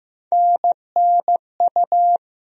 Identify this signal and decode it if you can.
Morse code, heard as a keyed tone, about 15 wpm, 700 Hz. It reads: NNU
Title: USA Call Sign Postfixes - 3 Letters 15wpm